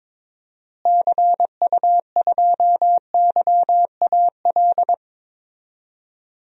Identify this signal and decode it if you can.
Morse code, heard as a keyed tone, about 21 wpm, 700 Hz. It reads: CU2YAL
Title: DXCC-100 Call Signs - 6 Characters - Repeated 2x - 22wpm